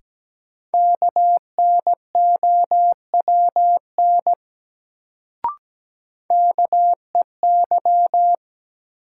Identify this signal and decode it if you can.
Morse code, heard as a keyed tone, about 17 wpm, 700 Hz. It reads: KNOWN KEY
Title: Sentences from Top 600 Words - New Word Drill - 17wpm